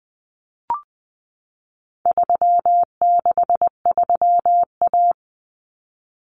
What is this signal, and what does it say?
Morse code, heard as a keyed tone, about 20 wpm, 700 Hz. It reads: 363A
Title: Common Cut Numbers - 4-Digit 20wpm